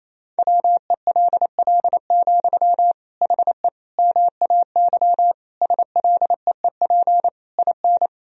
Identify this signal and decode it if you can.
Morse code, heard as a keyed tone, about 28 wpm, 700 Hz. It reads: WELL, HE MAY SLEEP IN
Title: Sentences from Top 1000 Words - Review Part 4 - 28wpm